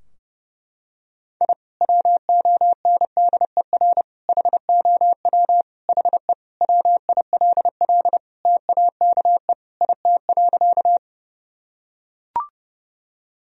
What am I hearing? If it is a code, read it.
Morse code, heard as a keyed tone, about 32 wpm, 700 Hz. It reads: I WONDER HOW HE WILL TAKE IT.
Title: Sentences from Top 1000 Words - Review Part 3 - 30wpm